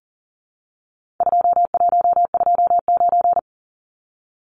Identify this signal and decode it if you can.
Morse code, heard as a keyed tone, about 38 wpm, 700 Hz. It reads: 2129
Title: Numbers - 4-Digit 40wpm